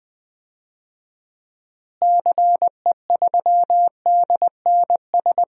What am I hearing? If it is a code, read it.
Morse code, heard as a keyed tone, about 20 wpm, 700 Hz. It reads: CE3DNS
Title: DXCC-100 Call Signs - 6 Characters - Repeated 2x - 20wpm